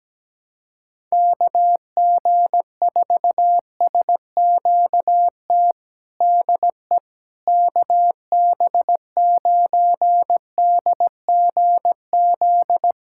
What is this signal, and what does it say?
Morse code, heard as a keyed tone, about 17 wpm, 700 Hz. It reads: KG4SQT DE KB9DGZ
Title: USA Callsign DE Callsign - 6 Characters - 17wpm